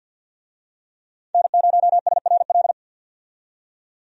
Morse code, heard as a keyed tone, about 46 wpm, 700 Hz. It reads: N0SRL
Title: USA Call Signs - 5 Characters - 50wpm